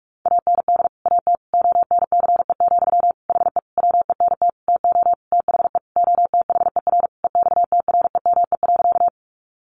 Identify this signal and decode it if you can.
Morse code, heard as a keyed tone, about 44 wpm, 700 Hz. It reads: AND AT ONCE, HE WENT TO THE OTHER EXTREME.
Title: Sentences from Top 1300 Words 45wpm